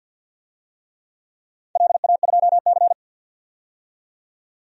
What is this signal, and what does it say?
Morse code, heard as a keyed tone, about 51 wpm, 700 Hz. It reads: LA2C
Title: DXCC-100 Call Signs - 4 Characters - 50wpm